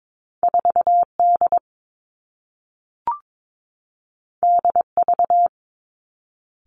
Morse code, heard as a keyed tone, about 22 wpm, 700 Hz. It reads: 4 D D V
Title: Two Challenging Letter-Number Combinations 22wpm